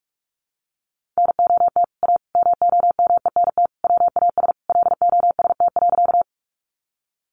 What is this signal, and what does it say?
Morse code, heard as a keyed tone, about 44 wpm, 700 Hz. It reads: NOT A MOMENT WAS LOST.
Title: Sentences from Top 1400 Words 45wpm